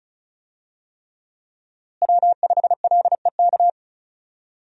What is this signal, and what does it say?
Morse code, heard as a keyed tone, about 34 wpm, 700 Hz. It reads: W5LEK
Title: USA Call Signs - 5 Characters - Repeated 3x - 35wpm